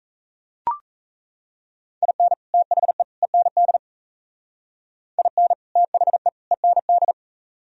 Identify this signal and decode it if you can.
Morse code, heard as a keyed tone, about 37 wpm, 700 Hz. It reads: IN THE END IN THE END